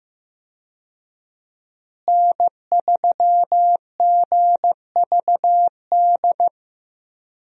Morse code, heard as a keyed tone, about 14 wpm, 700 Hz. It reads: N3GVD
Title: USA Call Signs - 5 Characters - Repeated 2x - 15wpm